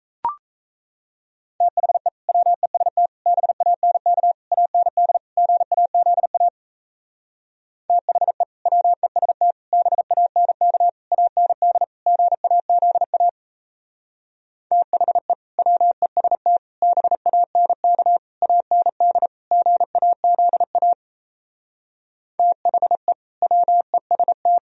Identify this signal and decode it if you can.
Morse code, heard as a keyed tone, about 38 wpm, 700 Hz. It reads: THE WEST BANK AND GAZA THE WEST BANK AND GAZA THE WEST BANK AND GAZA THE WEST